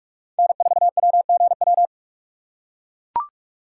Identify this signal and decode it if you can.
Morse code, heard as a keyed tone, about 42 wpm, 700 Hz. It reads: N4WGW